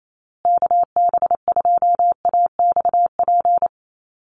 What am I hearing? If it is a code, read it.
Morse code, heard as a keyed tone, about 28 wpm, 700 Hz. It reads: KB2AXP
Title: USA Call Signs Repeated 2x 28wpm